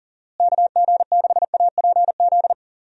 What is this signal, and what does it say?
Morse code, heard as a keyed tone, about 39 wpm, 700 Hz. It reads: KG6APZ
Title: USA Call Signs - 6 Characters - Repeated 3x - 40wpm